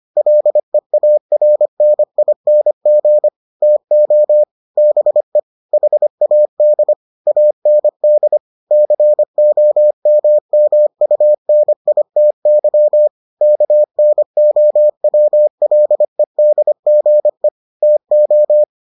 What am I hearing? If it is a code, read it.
Morse code, heard as a keyed tone, about 25 wpm, 600 Hz. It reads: LEARNING TO BE HAD AND COMMUNITY KNOWLEDGE TO